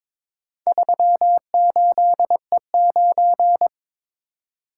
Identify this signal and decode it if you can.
Morse code, heard as a keyed tone, about 22 wpm, 700 Hz. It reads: 38E9
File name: practice / 22wpm / Four Letter-Number Combinations - 22wpm